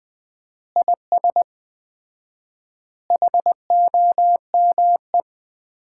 Morse code, heard as a keyed tone, about 20 wpm, 700 Hz. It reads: IS HOME
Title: Common QSO Phrases - 4x Word-Spacing - 20wpm